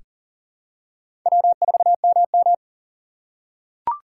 Morse code, heard as a keyed tone, about 40 wpm, 700 Hz. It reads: W4MM
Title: USA Call Signs - 4 Characters - 6x Character Spacing - 40wpm